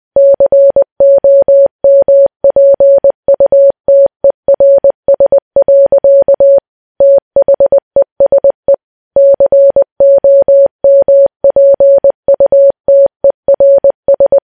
Morse code, heard as a keyed tone, about 20 wpm, 560 Hz. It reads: COMPUTERS. THESE COMPUTERS
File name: morse